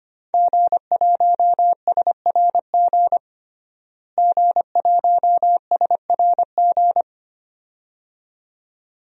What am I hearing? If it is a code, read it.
Morse code, heard as a keyed tone, about 25 wpm, 700 Hz. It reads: G1SRG G1SRG
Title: DXCC-100 Call Signs - 5 Characters - Repeated 2x - 25wpm